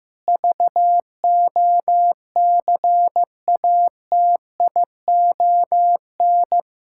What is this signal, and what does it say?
Morse code, heard as a keyed tone, about 15 wpm, 700 Hz. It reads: VOCATION
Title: Pangram Challenge v2 - 15wpm